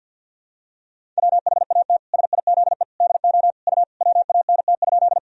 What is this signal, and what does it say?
Morse code, heard as a keyed tone, about 50 wpm, 700 Hz. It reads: WHAT SIZE DO U WANT?
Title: Sentences from Top 400 Words and Abbreviations 50wpm